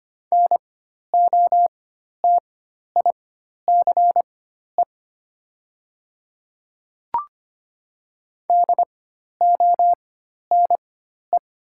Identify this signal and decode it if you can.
Morse code, heard as a keyed tone, about 25 wpm, spaced as for 12 wpm, 700 Hz. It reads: NOTICE DONE
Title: 4x Character Spacing - Top 500 Words - 25wpm